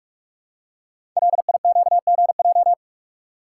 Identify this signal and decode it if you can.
Morse code, heard as a keyed tone, about 44 wpm, 690 Hz. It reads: LIQGJ
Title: Code Groups - 5 Letters - 45wpm